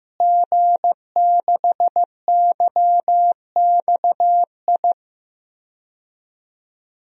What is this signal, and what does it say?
Morse code, heard as a keyed tone, about 15 wpm, 700 Hz. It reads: G6YXI
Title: DXCC-100 Call Signs - 5 Characters - Repeated 3x - 15wpm